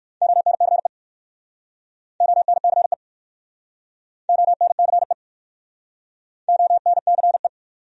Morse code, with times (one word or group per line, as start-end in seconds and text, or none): 0.2-0.9 ONCE
2.2-2.9 ONCE
4.3-5.1 ONCE
6.5-7.5 ONCE